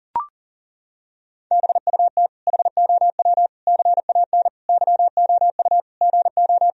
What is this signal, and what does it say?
Morse code, heard as a keyed tone, about 40 wpm, 700 Hz. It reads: BUT HOW CAN YOU GO
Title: Sentences from Top 1000 Words - Review Part 2 - 40wpm